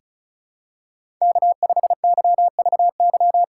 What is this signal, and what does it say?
Morse code, heard as a keyed tone, about 34 wpm, 700 Hz. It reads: K5YVY
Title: K1USN SST QSO Practice - 35wpm